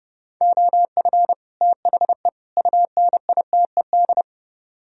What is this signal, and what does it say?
Morse code, heard as a keyed tone, about 37 wpm, 700 Hz. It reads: OF THE UNITED
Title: Speed-Racing - Sets of 3 Words 30wpm